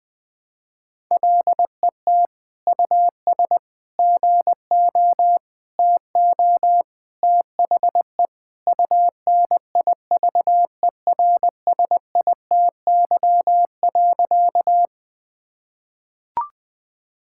Morse code, heard as a keyed tone, about 20 wpm, 700 Hz. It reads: LET US GO TO THE UNIVERSITY.
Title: Sentences from Top 1000 Words - Review Part 5 - 20wpm